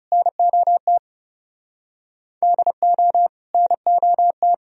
Morse code, heard as a keyed tone, about 38 wpm, 700 Hz. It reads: NOT DO NOT